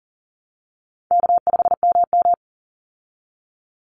Morse code, heard as a keyed tone, about 40 wpm, 700 Hz. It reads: K5MM
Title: Easy Call Signs Repeated 3x - 40wpm